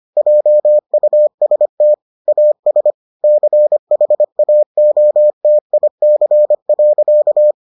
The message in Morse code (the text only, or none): JUST AS CHAOTIC.